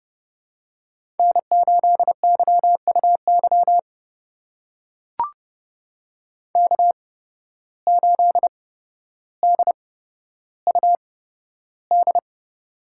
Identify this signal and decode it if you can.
Morse code, heard as a keyed tone, about 32 wpm, spaced as for 13 wpm, 700 Hz. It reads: N8YUY K8DUD
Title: USA Call Signs - 5 Characters - 8x Character Spacing - 30wpm